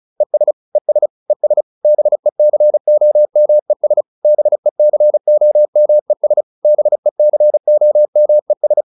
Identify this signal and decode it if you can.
Morse code, heard as a keyed tone, about 34 wpm, 600 Hz. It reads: ES ES ES BECOMES BECOMES BECOMES